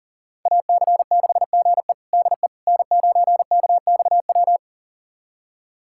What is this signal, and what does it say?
Morse code, heard as a keyed tone, about 37 wpm, 700 Hz. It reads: AC6GE DE N9KXW
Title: USA Callsign DE Callsign - 5 Characters - 40wpm